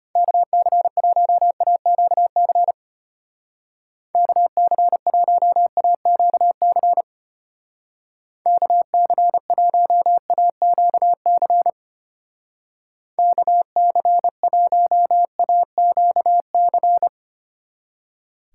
Morse code, heard as a keyed tone, about 32 wpm, 700 Hz. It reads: KC1AQC KC1AQC KC1AQC KC1AQC